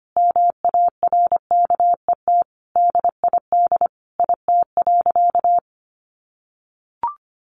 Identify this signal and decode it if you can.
Morse code, heard as a keyed tone, about 25 wpm, 700 Hz. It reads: MARKET DID IT.
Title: Sentences from Top 600 Words 25wpm